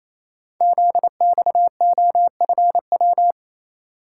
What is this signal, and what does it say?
Morse code, heard as a keyed tone, about 29 wpm, 700 Hz. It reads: ZXOFW
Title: Code Groups - 5 Letter:Numbers - 28wpm